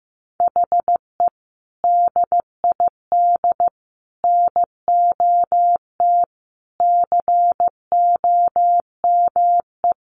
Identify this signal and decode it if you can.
Morse code, heard as a keyed tone, about 15 wpm, 700 Hz. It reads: HE DID NOT COME
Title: Sentences from Top 1200 Words 15wpm